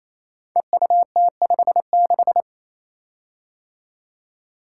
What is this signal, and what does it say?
Morse code, heard as a keyed tone, about 27 wpm, 700 Hz. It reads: EUT56
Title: Code Groups - 5 Letter:Numbers - 28wpm